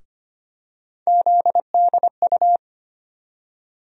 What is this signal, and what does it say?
Morse code, heard as a keyed tone, about 25 wpm, 700 Hz. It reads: ZDU
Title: USA Call Sign Postfixes - 3 Letters 25wpm